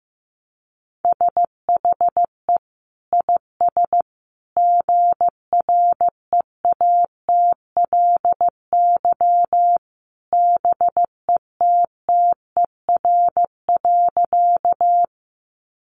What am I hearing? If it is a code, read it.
Morse code, heard as a keyed tone, about 15 wpm, 700 Hz. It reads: SHE IS GREATLY BETTER.
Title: Sentences from Top 1400 Words 15wpm